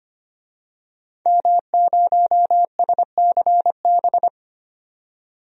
Morse code, heard as a keyed tone, about 25 wpm, 700 Hz. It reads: M0SCB
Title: DXCC-100 Call Signs - 5 Characters - 25wpm